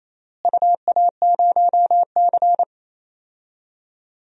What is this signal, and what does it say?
Morse code, heard as a keyed tone, about 28 wpm, 700 Hz. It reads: UA0C
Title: DXCC-100 Call Signs - 4 Characters - Repeated 3x - 28wpm